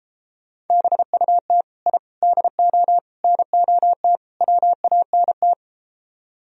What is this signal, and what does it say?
Morse code, heard as a keyed tone, about 33 wpm, 700 Hz. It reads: BUT I DO NOT WANT